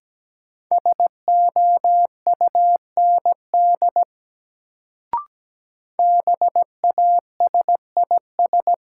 Morse code, heard as a keyed tone, about 17 wpm, 700 Hz. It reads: SOUND BASIS
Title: Sets of 3 Words - New Word Drill - 17wpm